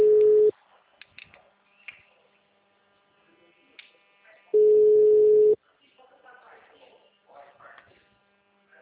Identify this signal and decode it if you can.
Ringback tone (Europe, 425 Hz), with speech.